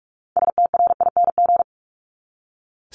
Morse code, heard as a keyed tone, about 44 wpm, 700 Hz. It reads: STRING
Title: Top 1501-1750 Words 45wpm